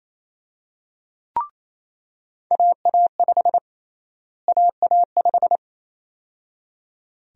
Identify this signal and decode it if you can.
Morse code, heard as a keyed tone, about 28 wpm, 700 Hz. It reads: AA5 AA5
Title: USA Call Sign Prefixes - Repeated 2x - 28wpm